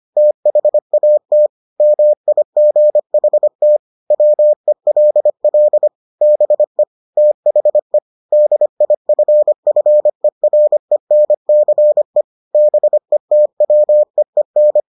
Morse code, heard as a keyed tone, about 25 wpm, 600 Hz. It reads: THAT MIGHT WELL BE THE DIFFERENCE BETWEEN